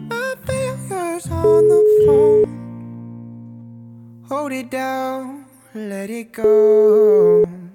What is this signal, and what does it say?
Ringback tone (Europe, 425 Hz), with speech.